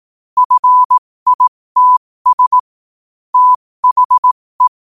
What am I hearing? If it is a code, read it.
Morse code, heard as a keyed tone, about 18 wpm, spaced as for 15 wpm, 1000 Hz. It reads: FITS THE
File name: qotdfiles / 15wpm